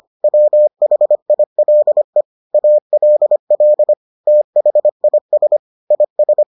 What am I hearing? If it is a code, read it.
Morse code, heard as a keyed tone, about 25 wpm, 600 Hz. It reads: WHILE ALL THIS IS